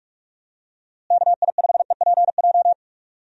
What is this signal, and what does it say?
Morse code, heard as a keyed tone, about 46 wpm, 700 Hz. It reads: KI5EPJ
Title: USA Call Signs - 6 Characters - Repeated 2x - 45wpm